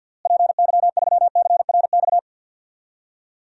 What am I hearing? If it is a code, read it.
Morse code, heard as a keyed tone, about 46 wpm, 700 Hz. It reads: PY3CRX
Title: DXCC-100 Call Signs - 6 Characters - Repeated 3x - 50wpm